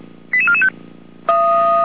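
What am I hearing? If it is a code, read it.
Five-tone selective call (ZVEI), heard as quick runs of five tones, tones of about 70 ms. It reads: 88445
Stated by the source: ZVEI five-tone selective call received on an SDR.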